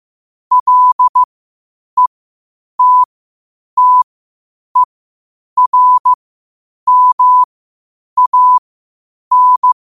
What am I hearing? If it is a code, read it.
Morse code, heard as a keyed tone, about 15 wpm, spaced as for 8 wpm, 1000 Hz. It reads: LETTERMAN
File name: qotdfiles / 8wpm